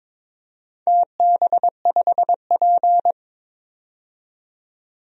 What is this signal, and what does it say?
Morse code, heard as a keyed tone, about 22 wpm, 700 Hz. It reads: TB5P
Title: DXCC-100 Call Signs - 4 Characters - 22wpm